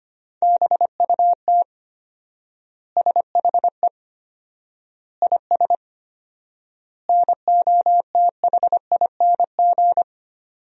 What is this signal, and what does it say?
Morse code, heard as a keyed tone, about 25 wpm, 700 Hz. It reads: BUT SHE IS NOTHING
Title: 4x Word-Spacing - Sentences from Top 500 Words - 25wpm